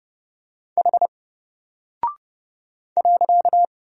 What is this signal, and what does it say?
Morse code, heard as a keyed tone, about 30 wpm, 700 Hz. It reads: H .
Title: Single Character (Letter:Number:Punctuation:Prosign) Alt 30wpm